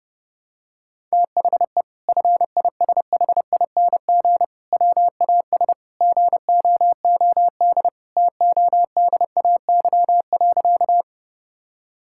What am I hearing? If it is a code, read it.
Morse code, heard as a keyed tone, about 30 wpm, 700 Hz. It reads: THE FISHING WAS GOOD TODAY.
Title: Sentences from Top 1000 Words - Review Part 5 - 30wpm